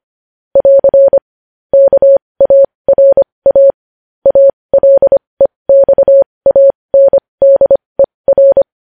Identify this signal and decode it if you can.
Morse code, heard as a keyed tone, about 25 wpm, spaced as for 20 wpm, 550 Hz. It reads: <AR> KARA ALEXANDER
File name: podcast / News headlines-1x-25x20